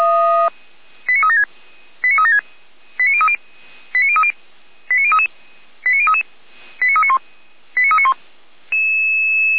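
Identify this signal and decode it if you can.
Five-tone selective call (ZVEI), heard as quick runs of five tones, tones of about 70 ms. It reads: 89376 89376 89030 89030 89033 89033 89391 89391
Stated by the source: ZVEI five-tone selective call received on SDR.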